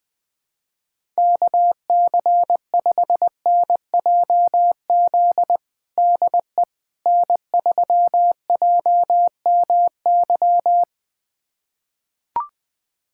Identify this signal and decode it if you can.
Morse code, heard as a keyed tone, about 21 wpm, 700 Hz. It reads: KC5NJZ DE N3JMY
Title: USA Callsign DE Callsign 20wpm